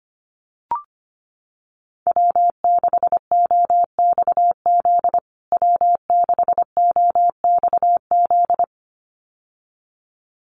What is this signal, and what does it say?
Morse code, heard as a keyed tone, about 25 wpm, 700 Hz. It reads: W6OXZ W6OXZ
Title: USA Call Signs Repeated 2x 25wpm